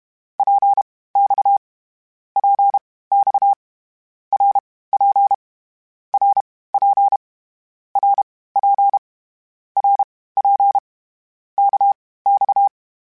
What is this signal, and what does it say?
Morse code, heard as a keyed tone, about 32 wpm, spaced as for 18 wpm, 800 Hz. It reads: PX PX RP RP RP RP KX
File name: Short Runs / Short Runs-mp3s / Balanced @30